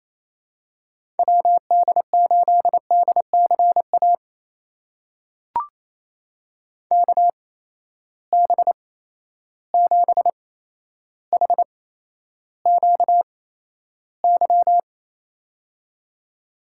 Morse code, heard as a keyed tone, about 28 wpm, spaced as for 12 wpm, 700 Hz. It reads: WD8DCA KB7HQY
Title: USA Call Signs - 6 Characters - 8x Character Spacing - 28wpm